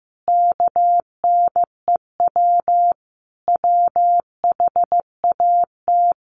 Morse code, heard as a keyed tone, about 15 wpm, 700 Hz. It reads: KNEW WHAT
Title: Sentences from Top 700 Words 15wpm